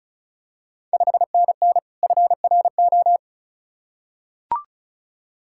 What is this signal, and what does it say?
Morse code, heard as a keyed tone, about 34 wpm, 690 Hz. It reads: 5NN FRO
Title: Ontario QSO Party Exchange 35wpm